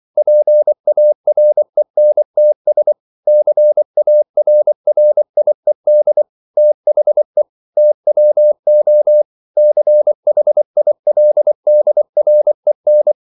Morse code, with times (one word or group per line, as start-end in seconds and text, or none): 0.2-2.9 PARENTS
3.3-6.2 CARRIED
6.6-7.4 THE
7.8-9.2 TWO
9.6-13.1 CHILDREN